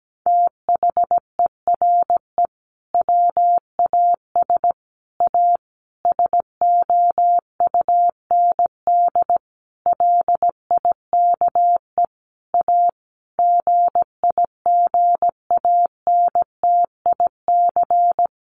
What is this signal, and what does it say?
Morse code, heard as a keyed tone, about 17 wpm, 700 Hz. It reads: THERE WAS A SOUND LIKE A GIGANTIC
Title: A Sound Of Thunder 17wpm